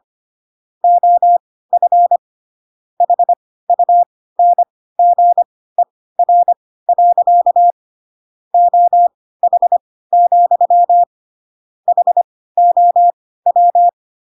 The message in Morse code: OF HUNGER. OH, HOW

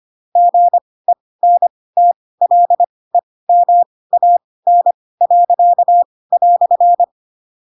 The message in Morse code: GENTLEMAN."

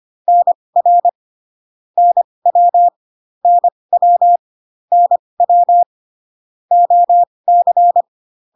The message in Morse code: NR NW NW NW OC

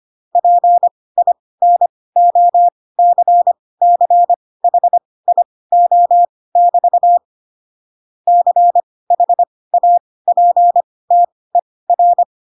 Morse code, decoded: PINOCCHIO<BT> CHAPTER

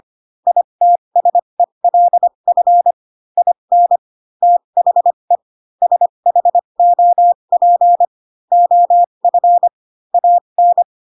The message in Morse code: ITSELF IN THE SHOP OF AN

